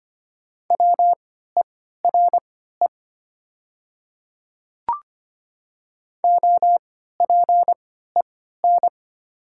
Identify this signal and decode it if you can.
Morse code, heard as a keyed tone, about 25 wpm, spaced as for 14 wpm, 690 Hz. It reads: WERE OPEN
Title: 3x Character Spacing - Top 500 Words - 25wpm